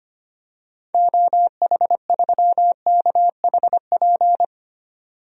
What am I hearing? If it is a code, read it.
Morse code, heard as a keyed tone, about 25 wpm, 700 Hz. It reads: OH3KHP
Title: DXCC-100 Call Signs - 6 Characters - 25wpm